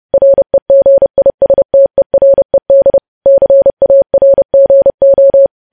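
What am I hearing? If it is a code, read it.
Morse code, heard as a keyed tone, about 30 wpm, 550 Hz. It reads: REGISTERED CARGO
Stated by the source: News headlines sent as Morse practice at 30.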